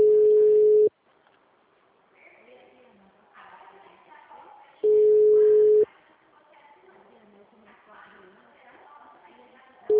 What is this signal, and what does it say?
Ringback tone (Europe, 425 Hz), with speech.
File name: out- -2016- - - .8879